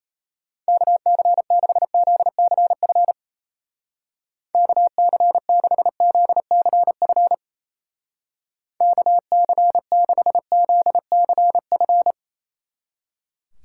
Morse code, heard as a keyed tone, about 33 wpm, 700 Hz. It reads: KC6ZCF KC6ZCF KC6ZCF
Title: Speed Racing - USA 2x3 Call Signs 28wpm